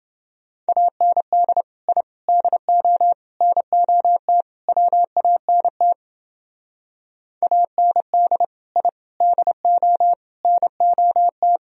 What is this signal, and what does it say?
Morse code, heard as a keyed tone, about 28 wpm, 700 Hz. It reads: AND I DO NOT WANT AND I DO NOT